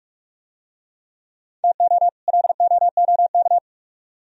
Morse code, heard as a keyed tone, about 44 wpm, 700 Hz. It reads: TO LOOK